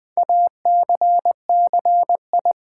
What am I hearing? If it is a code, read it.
Morse code, heard as a keyed tone, about 20 wpm, 700 Hz. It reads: ACCI
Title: Build A Word - 2 Letters At A Time - 8-Letter Words - 20wpm